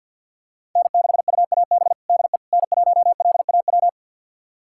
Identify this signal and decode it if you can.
Morse code, heard as a keyed tone, about 52 wpm, 700 Hz. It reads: N6UAB DE N1LAW
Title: USA Callsign DE Callsign - 5 Characters - 50wpm